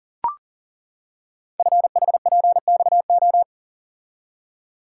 Morse code, heard as a keyed tone, about 41 wpm, 690 Hz. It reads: FHPXO
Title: Code Groups - 5 Letters - 40wpm